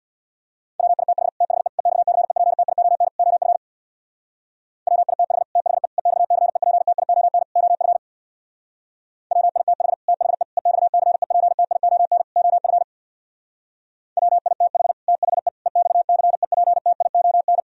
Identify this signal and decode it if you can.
Morse code, heard as a keyed tone, about 64 wpm, 700 Hz. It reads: WITH THE EXCEPTION OF WITH THE EXCEPTION OF WITH THE EXCEPTION OF WITH THE EXCEPTION